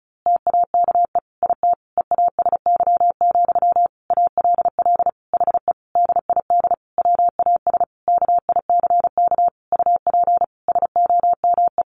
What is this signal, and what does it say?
Morse code, heard as a keyed tone, about 34 wpm, 700 Hz. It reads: TAKE IT EASY, ALL HE DID WAS KICK UP SOME
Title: A Sound Of Thunder 35wpm